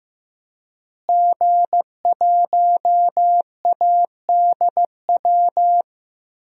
Morse code, heard as a keyed tone, about 15 wpm, 700 Hz. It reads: G1ADW
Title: DXCC-100 Call Signs - 5 Characters - Repeated 2x - 15wpm